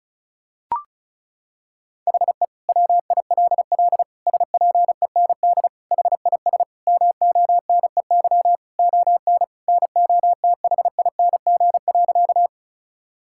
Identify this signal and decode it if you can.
Morse code, heard as a keyed tone, about 34 wpm, 700 Hz. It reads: HE WILL SPEND HIS MONEY ON NOTHING.
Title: Sentences from Top 1000 Words - Review Part 5 - 35wpm